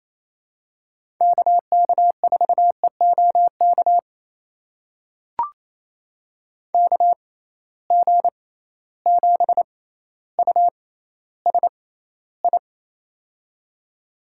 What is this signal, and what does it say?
Morse code, heard as a keyed tone, about 28 wpm, spaced as for 14 wpm, 700 Hz. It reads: KK4EOK KG7USI
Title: USA Call Signs - 6x Character Spacing - 28wpm